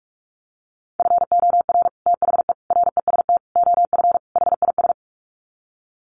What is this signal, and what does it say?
Morse code, heard as a keyed tone, about 44 wpm, 700 Hz. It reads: FOR THE REST OF HIS
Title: Sets of 5 Words 45wpm